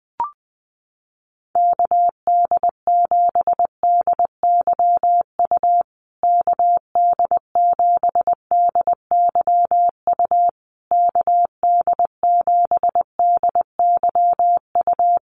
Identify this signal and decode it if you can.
Morse code, heard as a keyed tone, about 20 wpm, 700 Hz. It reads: KD7DYU KD7DYU KD7DYU
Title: USA Call Signs Repeated 3x 20wpm